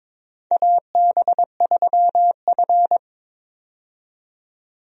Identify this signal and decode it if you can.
Morse code, heard as a keyed tone, about 22 wpm, 700 Hz. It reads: AB3F
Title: USA Call Signs - 4 Characters - 22wpm